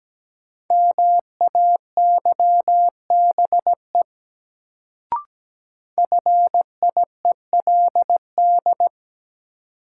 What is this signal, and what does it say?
Morse code, heard as a keyed tone, about 17 wpm, 700 Hz. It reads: MAYBE FIELD
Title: Sentences from Top 1000 Words - New Word Drill - 17wpm